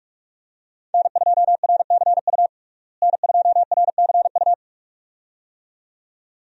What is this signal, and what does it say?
Morse code, heard as a keyed tone, about 45 wpm, 700 Hz. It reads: N2RCU N2RCU
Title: USA Call Signs - 5 Characters - Repeated 2x - 45wpm